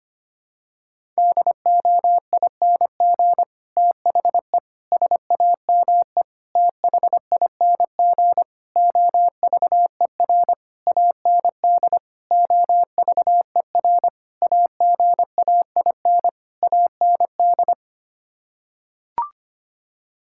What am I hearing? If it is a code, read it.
Morse code, heard as a keyed tone, about 25 wpm, 700 Hz. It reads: DOING THE SAME THING OVER AND OVER AGAIN AND
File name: Sets of 9 Words - 25wpm